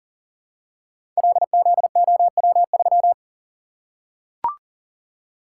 Morse code, heard as a keyed tone, about 40 wpm, 700 Hz. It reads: LZOW3
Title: Code Groups - 5 Letter:Numbers - 40wpm